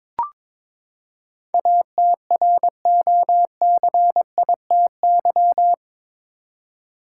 Morse code, heard as a keyed tone, about 22 wpm, 700 Hz. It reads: ATROCITY
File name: practice / 22wpm / Eight Letter Words - 22wpm